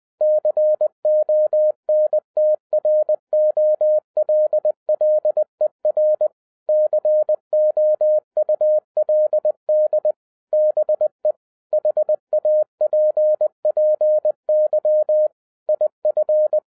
Morse code, heard as a keyed tone, about 20 wpm, 610 Hz. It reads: CONTROLLER COULD BE HAPPY IF
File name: 2012-01-mp3 / n16-THROUGH EIGHT MONTHS-20wpm